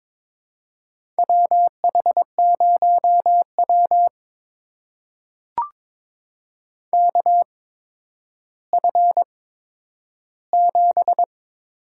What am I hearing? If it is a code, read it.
Morse code, heard as a keyed tone, about 22 wpm, spaced as for 11 wpm, 700 Hz. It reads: WH0W KF7
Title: USA Call Signs - 8x Character Spacing - 22wpm